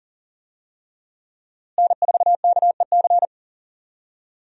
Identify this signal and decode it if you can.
Morse code, heard as a keyed tone, about 40 wpm, 700 Hz. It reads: N4KEC
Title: USA Call Signs - 5 Characters - Repeated 3x - 40wpm